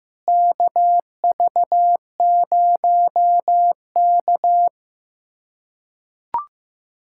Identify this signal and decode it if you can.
Morse code, heard as a keyed tone, about 15 wpm, 700 Hz. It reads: KV0K
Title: USA Callsign DE Callsign - 4 Characters - 15wpm